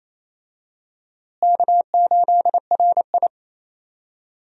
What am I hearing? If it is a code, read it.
Morse code, heard as a keyed tone, about 27 wpm, 700 Hz. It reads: K8RI
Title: USA Call Signs - 4 Characters - Repeated 3x - 28wpm